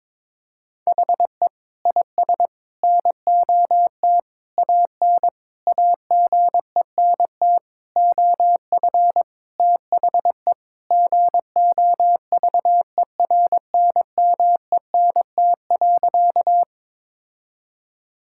Morse code, heard as a keyed tone, about 22 wpm, 700 Hz. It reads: HE IS NOT AN AGENT OF THE GOVERNMENT.
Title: Sentences from Top 1600 Words 22wpm